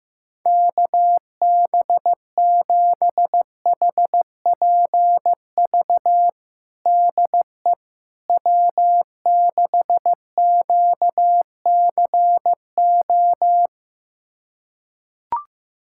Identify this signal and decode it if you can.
Morse code, heard as a keyed tone, about 15 wpm, 700 Hz. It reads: KB7HPV DE W6QCO
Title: USA Callsign DE Callsign 15wpm